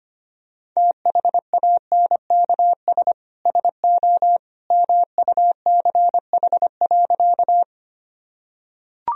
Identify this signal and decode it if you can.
Morse code, heard as a keyed tone, about 25 wpm, 700 Hz. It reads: THANKS SO MUCH.
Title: Sentences from Top 1000 Words - Review Part 3 - 25wpm